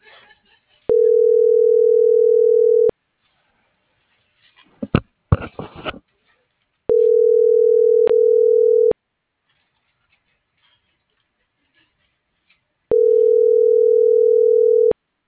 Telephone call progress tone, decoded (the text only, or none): ringback tone